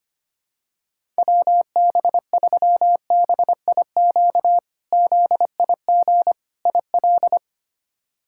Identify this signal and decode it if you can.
Morse code, heard as a keyed tone, about 25 wpm, 700 Hz. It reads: WB3BIQ ZIG IL
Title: K1USN SST QSO Practice - 25wpm